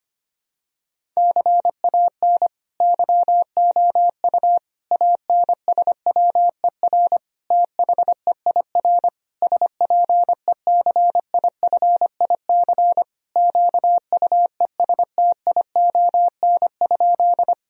Morse code, heard as a keyed tone, about 25 wpm, 700 Hz. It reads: CAN YOU ANSWER THEIR SPECIFIC QUESTION?
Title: Sentences from Top 1000 Words - Review Part 3 - 25wpm